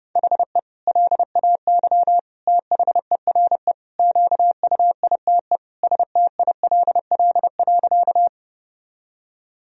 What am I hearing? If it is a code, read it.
Morse code, heard as a keyed tone, about 32 wpm, 700 Hz. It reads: HE LAY THERE QUITE STILL.
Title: Sentences from Top 1000 Words - Review Part 2 - 30wpm